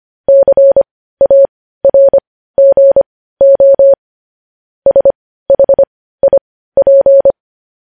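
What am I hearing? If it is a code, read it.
Morse code, heard as a keyed tone, about 25 wpm, spaced as for 15 wpm, 550 Hz. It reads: CARGO SHIP
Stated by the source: Morse code news headlines sent at 15 WPM.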